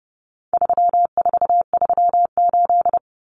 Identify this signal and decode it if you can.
Morse code, heard as a keyed tone, about 30 wpm, 700 Hz. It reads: 3438
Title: Numbers - 4-Digit 30wpm